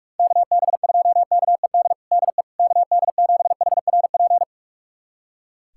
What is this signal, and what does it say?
Morse code, heard as a keyed tone, about 46 wpm, 700 Hz. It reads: KB2KED DE KD7HRP
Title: USA Callsign DE Callsign 45wpm